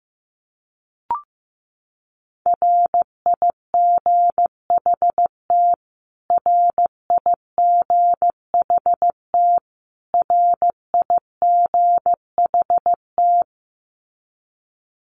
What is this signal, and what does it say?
Morse code, heard as a keyed tone, about 15 wpm, 700 Hz. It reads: RIGHT RIGHT RIGHT
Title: Top 500 Words Repeated 15wpm